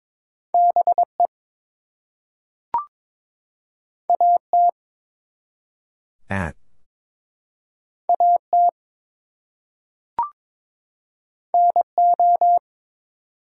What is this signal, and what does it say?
Morse code, heard as a keyed tone, about 22 wpm, 700 Hz. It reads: BE AT AT NO
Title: Two Letter Words - 22wpm